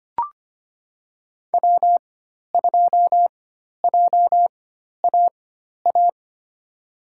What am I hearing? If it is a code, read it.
Morse code, heard as a keyed tone, about 25 wpm, spaced as for 12 wpm, 700 Hz. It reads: W2JAA
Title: USA Call Signs - 5 Characters - 4x Character Spacing - 25wpm